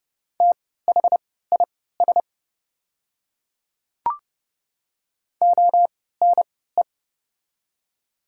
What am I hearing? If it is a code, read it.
Morse code, heard as a keyed tone, about 30 wpm, spaced as for 17 wpm, 700 Hz. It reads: THIS ONE
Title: 3x Character Spacing - Top 100 Words - 30wpm